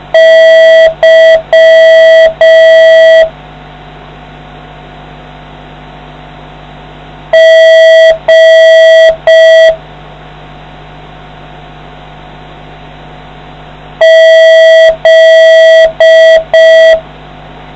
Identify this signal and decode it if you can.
Morse code, heard as a keyed tone, about 5 wpm, 650 Hz. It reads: Y G Z